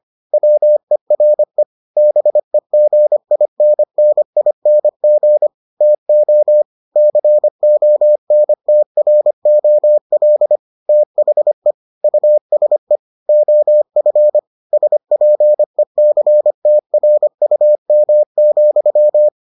Morse code, heard as a keyed tone, about 25 wpm, 600 Hz. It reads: WERE BEGINNING TO CONTROL THE USE OF SPECTRUM,